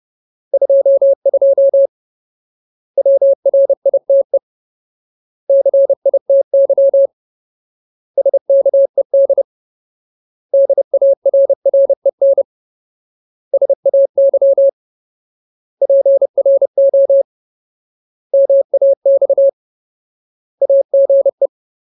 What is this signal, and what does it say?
Morse code, heard as a keyed tone, about 30 wpm, 550 Hz. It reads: 22 WRITE CITY SKED DARREN SAY PRO MAX AGE
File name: Ultimate Word List / 30WPM Ultimate